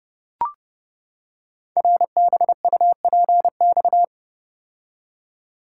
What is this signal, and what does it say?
Morse code, heard as a keyed tone, about 30 wpm, 700 Hz. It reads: RBUPX
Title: Code Groups - 5 Letters - 30wpm